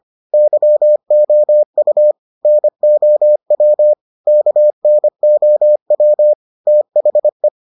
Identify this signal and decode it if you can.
Morse code, heard as a keyed tone, about 25 wpm, 600 Hz. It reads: YOU NOW KNOW THE